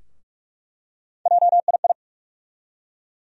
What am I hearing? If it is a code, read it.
Morse code, heard as a keyed tone, about 46 wpm, 700 Hz. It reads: JII
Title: USA Call Sign Postfixes - 3 Letters 45wpm